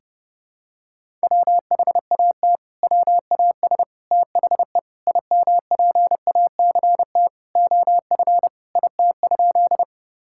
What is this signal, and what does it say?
Morse code, heard as a keyed tone, about 32 wpm, 700 Hz. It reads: WHAT WAS THE IMPACT OF IT?
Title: Sentences from Top 1000 Words - Review Part 6 - 30wpm